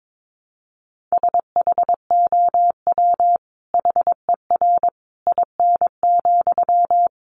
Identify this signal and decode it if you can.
Morse code, heard as a keyed tone, about 22 wpm, 700 Hz. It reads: SHOW HER IN,
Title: Sentences from Top 1100 Words 22wpm